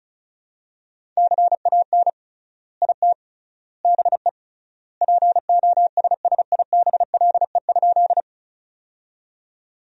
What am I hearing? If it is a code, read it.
Morse code, heard as a keyed tone, about 34 wpm, 700 Hz. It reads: CAN IT BE POSSIBLE?
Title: 3x Word-Spacing - Sentences from Top 500 Words - 35wpm